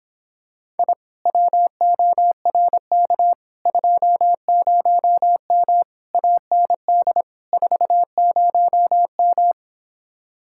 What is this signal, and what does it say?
Morse code, heard as a keyed tone, about 26 wpm, 700 Hz. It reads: I WORK 20M AND 40M